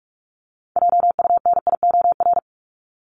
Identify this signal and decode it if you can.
Morse code, heard as a keyed tone, about 44 wpm, 700 Hz. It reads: JUNIOR
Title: Top 751-1000 Words 45wpm